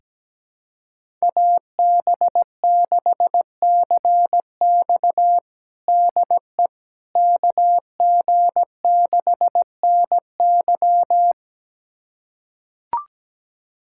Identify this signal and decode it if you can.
Morse code, heard as a keyed tone, about 17 wpm, 700 Hz. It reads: AB6CX DE KG6NY
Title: USA Callsign DE Callsign - 5 Characters - 17wpm